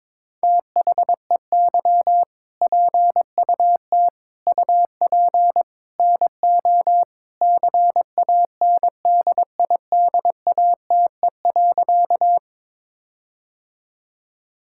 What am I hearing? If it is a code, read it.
Morse code, heard as a keyed tone, about 22 wpm, 700 Hz. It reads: THEY PUT UP NO CANDIDATE.
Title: Sentences from Top 1600 Words 22wpm